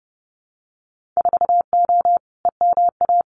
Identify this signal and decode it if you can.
Morse code, heard as a keyed tone, about 30 wpm, 700 Hz. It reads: 4O EMA
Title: Winter Field Day Exchanges 30wpm